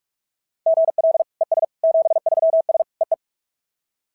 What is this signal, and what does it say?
Morse code, heard as a keyed tone, about 44 wpm, 690 Hz. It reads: GL ES 73S EE
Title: SOTA QSO Practice 45wpm